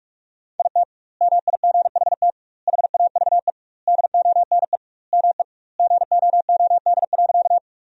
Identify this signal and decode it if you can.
Morse code, heard as a keyed tone, about 46 wpm, 700 Hz. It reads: IT MIGHT HAVE DONE ME GOOD.
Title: 2x Word-Spacing - Sentences from Top 500 Words - 45wpm